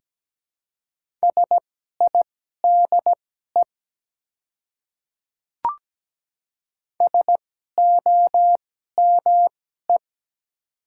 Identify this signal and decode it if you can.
Morse code, heard as a keyed tone, about 17 wpm, spaced as for 12 wpm, 690 Hz. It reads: SIDE SOME
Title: 2x Character Spacing - Top 100 Words - 17wpm